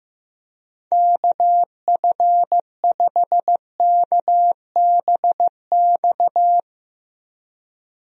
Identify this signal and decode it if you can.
Morse code, heard as a keyed tone, about 15 wpm, 700 Hz. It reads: KF5KBX
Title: USA Call Signs - 6 Characters - Repeated 2x - 15wpm